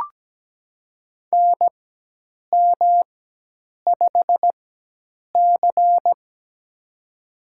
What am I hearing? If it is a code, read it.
Morse code, heard as a keyed tone, about 17 wpm, spaced as for 8 wpm, 700 Hz. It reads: NM5C
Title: USA Call Signs - 4x Character Spacing - 17wpm